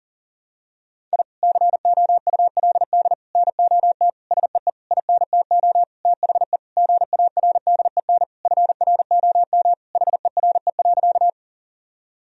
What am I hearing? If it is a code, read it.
Morse code, heard as a keyed tone, about 40 wpm, 700 Hz. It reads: I COULD NOT SEE INTO THE GARDEN FROM HERE.
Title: Sentences from Top 1000 Words - Review Part 4 - 40wpm